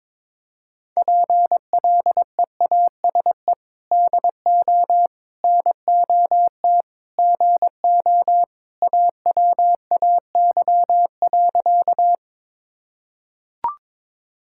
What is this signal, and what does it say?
Morse code, heard as a keyed tone, about 22 wpm, 700 Hz. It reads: PLEASE DO NOT GO AWAY.
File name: practice / 22wpm / Sentences with 4-Word Phrases - 22wpm